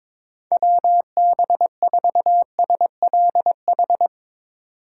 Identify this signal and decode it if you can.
Morse code, heard as a keyed tone, about 22 wpm, 700 Hz. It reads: WB4SLH
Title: Speed Racing - USA 2x3 Call Signs 22wpm